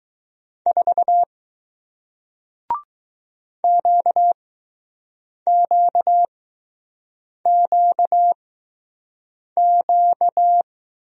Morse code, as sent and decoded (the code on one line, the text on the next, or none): ....- / --.- --.- --.- --.-
4 QQQQ